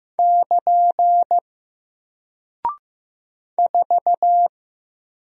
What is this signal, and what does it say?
Morse code, heard as a keyed tone, about 15 wpm, 700 Hz. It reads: <KN> 4
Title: Single Character (Letter:Number:Punctuation:Prosign) 15wpm